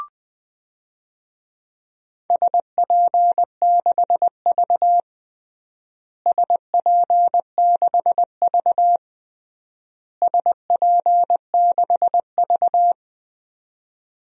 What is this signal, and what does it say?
Morse code, heard as a keyed tone, about 20 wpm, 700 Hz. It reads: SP6V SP6V SP6V
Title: DXCC-100 Call Signs - 4 Characters - Repeated 3x - 20wpm